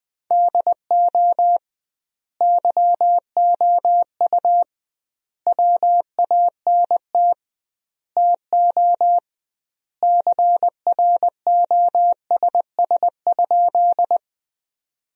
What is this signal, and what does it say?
Morse code, heard as a keyed tone, about 20 wpm, 700 Hz. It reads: DO YOU WANT TO CROSS?
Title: 2x Word-Spacing - Sentences from Top 300 Words - 20wpm